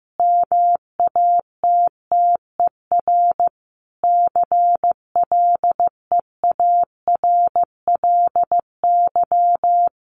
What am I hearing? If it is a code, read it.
Morse code, heard as a keyed tone, about 15 wpm, 700 Hz. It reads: MATTER CLEARLY
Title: Sentences from Top 1300 Words 15wpm